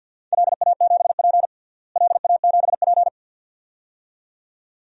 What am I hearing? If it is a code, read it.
Morse code, heard as a keyed tone, about 49 wpm, 700 Hz. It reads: LA7P LA7P
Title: DXCC-100 Call Signs - 4 Characters - Repeated 2x - 50wpm